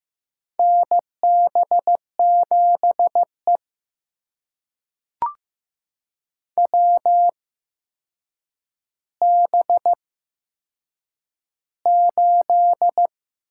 Morse code, heard as a keyed tone, about 15 wpm, spaced as for 7 wpm, 700 Hz. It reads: NB7E WB8
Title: USA Call Signs - 8x Character Spacing - 15wpm